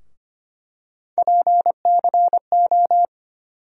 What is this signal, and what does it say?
Morse code, heard as a keyed tone, about 25 wpm, 700 Hz. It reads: PCO
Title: USA Call Sign Postfixes - 3 Letters 25wpm